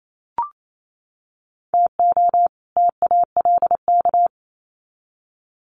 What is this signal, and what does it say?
Morse code, heard as a keyed tone, about 28 wpm, 700 Hz. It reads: TO TALK
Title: Sets of 2 Words 28wpm